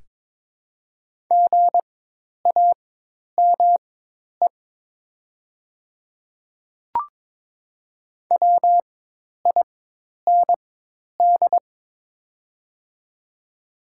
Morse code, heard as a keyed tone, about 22 wpm, spaced as for 10 wpm, 700 Hz. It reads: GAME WIND
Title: 4x Character Spacing - Top 500 Words - 22wpm